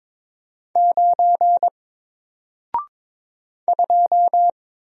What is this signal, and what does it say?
Morse code, heard as a keyed tone, about 22 wpm, 700 Hz. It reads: 9 2
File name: Single-Letter-Number-22wpm